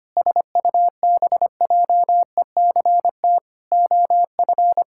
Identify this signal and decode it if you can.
Morse code, heard as a keyed tone, about 25 wpm, 700 Hz. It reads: SUBJECT OF
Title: Sentences from Top 1600 Words 25wpm